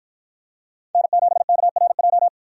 Morse code, heard as a keyed tone, about 53 wpm, 700 Hz. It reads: N7KRJ